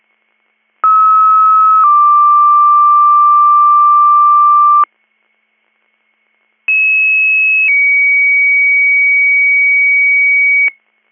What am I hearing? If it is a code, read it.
Two-tone page, heard as a one-second tone, then a three-second tone, about 1290 then 1180 Hz, about 2600 then 2250 Hz. It is heard from 0.8 to 10.7 s.